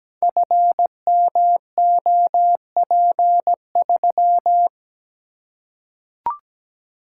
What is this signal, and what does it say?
Morse code, heard as a keyed tone, about 17 wpm, 700 Hz. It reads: FMOP3
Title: Code Groups - 5 Letter:Numbers - 17wpm